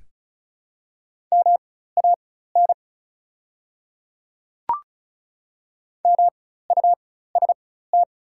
Morse code, heard as a keyed tone, about 34 wpm, spaced as for 16 wpm, 700 Hz. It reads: MAN MUST
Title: 4x Character Spacing - Top 200 Words - 35wpm